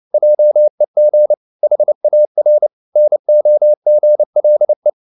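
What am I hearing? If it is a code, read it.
Morse code, heard as a keyed tone, about 29 wpm, 600 Hz. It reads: JEG HAR NOGLE
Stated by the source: Fast (QRQ) Morse at 29 wpm, from CW med Gnister.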